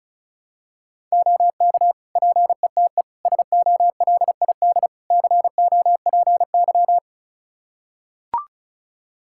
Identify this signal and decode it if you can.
Morse code, heard as a keyed tone, about 36 wpm, 700 Hz. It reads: OK PETE SOLID COPY
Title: Practice QSO - Exchange 3 - More Info - 35wpm